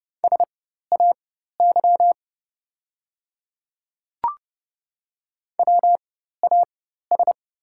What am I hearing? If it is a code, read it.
Morse code, heard as a keyed tone, about 30 wpm, spaced as for 14 wpm, 700 Hz. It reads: SAY WAS
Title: 4x Character Spacing - Top 200 Words - 30wpm